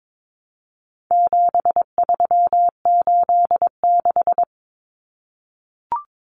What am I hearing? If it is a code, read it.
Morse code, heard as a keyed tone, about 22 wpm, 700 Hz. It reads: 7386
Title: Common Cut Numbers - 4-Digit 22wpm